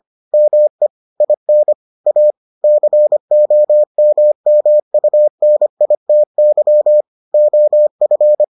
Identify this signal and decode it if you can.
Morse code, heard as a keyed tone, about 25 wpm, 600 Hz. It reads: ME IN A COMMUNITY OF